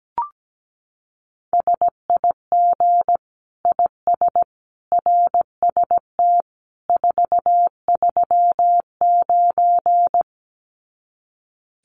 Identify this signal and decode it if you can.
Morse code, heard as a keyed tone, about 17 wpm, 700 Hz. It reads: SIG IS RST 439
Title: Common QSO Phrases 17wpm